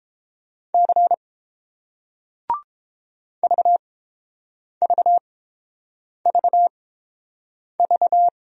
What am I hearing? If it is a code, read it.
Morse code, heard as a keyed tone, about 30 wpm, spaced as for 7 wpm, 700 Hz. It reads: C VVVV